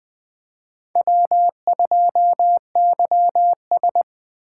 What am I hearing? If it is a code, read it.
Morse code, heard as a keyed tone, about 21 wpm, 700 Hz. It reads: W2YS
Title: Speed Racing - USA Call Signs 17wpm